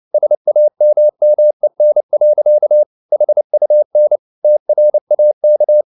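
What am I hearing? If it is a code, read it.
Morse code, heard as a keyed tone, about 29 wpm, 600 Hz. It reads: SAMMEN. HUN TRAK
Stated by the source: Fast (QRQ) Morse at 29 wpm, from CW med Gnister.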